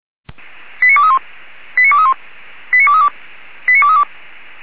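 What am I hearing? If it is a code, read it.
Five-tone selective call (ZVEI), heard as quick runs of five tones, tones of about 70 ms. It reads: 89231 89231 89232 89232
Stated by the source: Five-tone ZVEI selective call received by SDR, archived on the feuerwehr-orlamuende.org fire brigade server.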